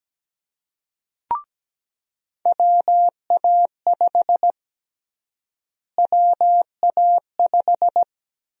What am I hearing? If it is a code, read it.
Morse code, heard as a keyed tone, about 17 wpm, 700 Hz. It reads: WA5 WA5
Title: USA Call Sign Prefixes - Repeated 2x - 17wpm